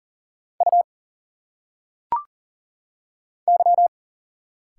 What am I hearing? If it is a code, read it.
Morse code, heard as a keyed tone, about 42 wpm, 700 Hz. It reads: U Y
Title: Single Letters - Phonetic Pronunciation 40wpm